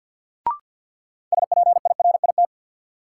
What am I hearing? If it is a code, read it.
Morse code, heard as a keyed tone, about 52 wpm, 700 Hz. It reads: SPIRIT
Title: Sentences from Top 1400 Words - New Word Drill - 50wpm